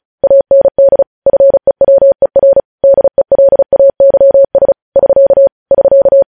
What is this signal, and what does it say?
Morse code, heard as a keyed tone, about 36 wpm, 550 Hz. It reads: AND FEWER DELAYS <SK> <SK>
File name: podcast / News headlines-1x-35x35